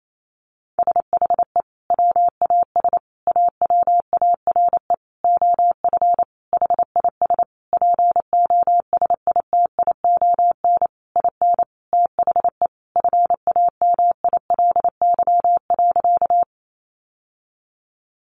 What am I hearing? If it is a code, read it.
Morse code, heard as a keyed tone, about 28 wpm, 700 Hz. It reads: SHE WAS AWARE OF HIS POSITION IN THE FAMILY.
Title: Sentences from Top 1200 Words 28wpm